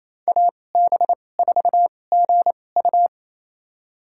Morse code, heard as a keyed tone, about 28 wpm, spaced as for 20 wpm, 700 Hz. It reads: AB4GU
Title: USA Call Signs - 5 Characters - 2x Character Spacing - 28wpm